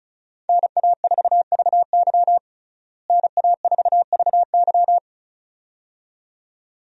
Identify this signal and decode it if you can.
Morse code, heard as a keyed tone, about 36 wpm, 700 Hz. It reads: NA4VY NA4VY
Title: USA Call Signs - 5 Characters - Repeated 2x - 35wpm